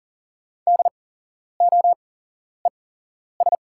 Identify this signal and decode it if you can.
Morse code, heard as a keyed tone, about 40 wpm, spaced as for 11 wpm, 700 Hz. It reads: DOES
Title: 8x Character Spacing - Top 200 Words - 40wpm